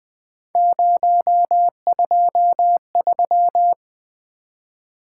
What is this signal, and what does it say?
Morse code, heard as a keyed tone, about 20 wpm, 700 Hz. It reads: 023
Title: Three Letter-Number Combinations - 20wpm